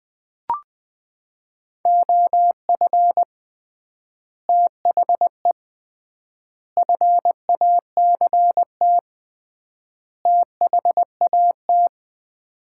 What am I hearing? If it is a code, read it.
Morse code, heard as a keyed tone, about 20 wpm, 700 Hz. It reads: OF THE FACT THAT
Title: 3x Word-Spacing - Sets of 4 Words - 20wpm